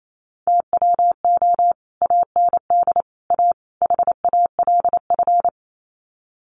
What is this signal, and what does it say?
Morse code, heard as a keyed tone, about 28 wpm, 700 Hz. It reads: TWO AND A HALF
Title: Sets of 4 Words 28wpm